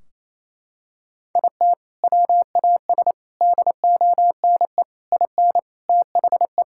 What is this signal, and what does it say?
Morse code, heard as a keyed tone, about 28 wpm, 700 Hz. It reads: IT WAS DONE IN THE
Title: Sentences from Top 1600 Words 28wpm